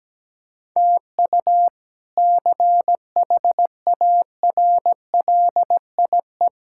Morse code, heard as a keyed tone, about 17 wpm, 700 Hz. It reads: TU CHARLIE
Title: K1USN SST QSO Practice - 17wpm